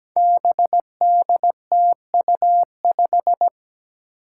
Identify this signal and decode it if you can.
Morse code, heard as a keyed tone, about 17 wpm, 700 Hz. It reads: BDTU5
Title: Five Challenging Letter-Number Combinations 17wpm